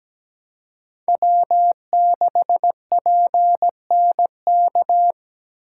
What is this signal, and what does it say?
Morse code, heard as a keyed tone, about 17 wpm, 700 Hz. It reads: W6PNK
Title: USA Call Signs - 5 Characters - 3x Character Spacing - 17wpm